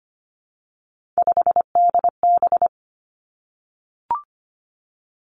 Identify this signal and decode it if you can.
Morse code, heard as a keyed tone, about 25 wpm, 700 Hz. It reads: 5DB
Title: Three Challenging Letter-Number Combinations 25wpm